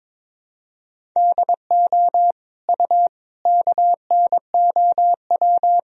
Morse code, heard as a keyed tone, about 21 wpm, 700 Hz. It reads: DO U KNOW
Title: Sentences from Top 100 Words and Abbreviations 22wpm